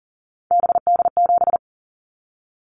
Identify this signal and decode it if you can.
Morse code, heard as a keyed tone, about 40 wpm, 690 Hz. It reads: BD7
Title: Callsign Prefixes for Top 100 DXCC - 40wpm